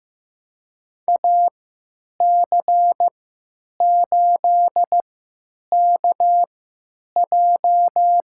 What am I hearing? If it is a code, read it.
Morse code, heard as a keyed tone, about 15 wpm, spaced as for 9 wpm, 700 Hz. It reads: AC8KJ
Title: USA Call Signs - 5 Characters - 3x Character Spacing - 15wpm